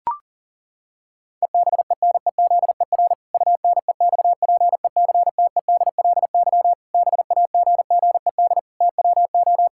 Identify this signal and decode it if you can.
Morse code, heard as a keyed tone, about 40 wpm, 700 Hz. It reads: EBENEZER UNEXPECTEDLY BAGGED TWO
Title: Pangram Challenge v2 - 40wpm